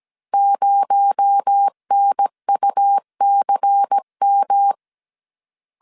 Morse code, heard as a keyed tone, about 17 wpm, 800 Hz. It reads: 0NUCM
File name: CW / DET / 1 6 1500C S B 16wpm 2619